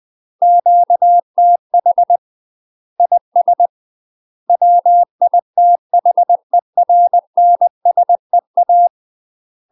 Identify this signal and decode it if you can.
Morse code, heard as a keyed tone, about 20 wpm, 700 Hz. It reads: QTH IS WITHERNSEA
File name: UK MASTER QSOs TRAINING / 20WPM / UK MASTER QSOs 0053